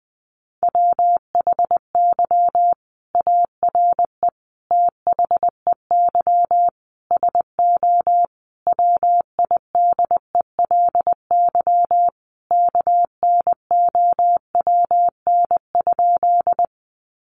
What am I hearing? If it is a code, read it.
Morse code, heard as a keyed tone, about 20 wpm, 700 Hz. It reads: WHY ARE THEY SO WIDELY KNOWN?
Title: Sentences from Top 1200 Words 20wpm